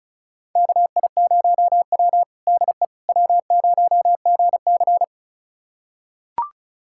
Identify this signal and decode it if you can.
Morse code, heard as a keyed tone, about 34 wpm, 700 Hz. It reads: KI0W DE W0GC
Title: USA Callsign DE Callsign - 4 Characters - 35wpm